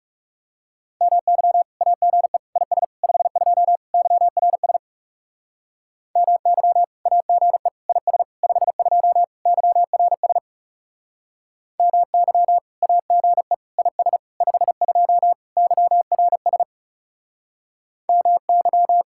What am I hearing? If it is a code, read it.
Morse code, heard as a keyed tone, about 36 wpm, 700 Hz. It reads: MY AGE IS 52 YRS MY AGE IS 52 YRS MY AGE IS 52 YRS MY
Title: Speed-Racing Common QSO Phrases Encore - 30wpm